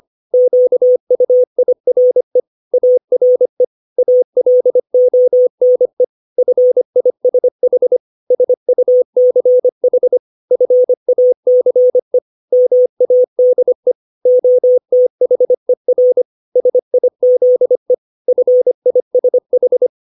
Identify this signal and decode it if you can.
Morse code, heard as a keyed tone, about 25 wpm, 500 Hz. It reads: QUIRE ARE ALONE FISH SUCH FACE MADE OTHER SIZE FISH
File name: 632 words / 632 Words 25WPM